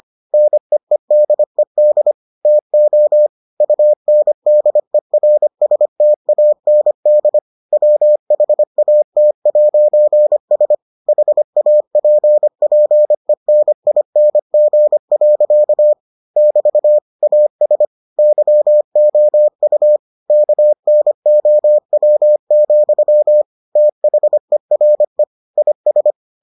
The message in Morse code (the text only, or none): NEEDED TO UNDERSTAND WHAT'S HAPPENING. <BT> AS YOU KNOW, THERE IS